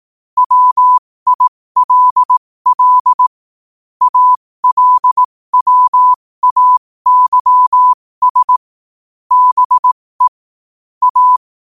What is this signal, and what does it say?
Morse code, heard as a keyed tone, about 18 wpm, spaced as for 15 wpm, 1000 Hz. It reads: WILL ALWAYS BE A
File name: qotdfiles / 15wpm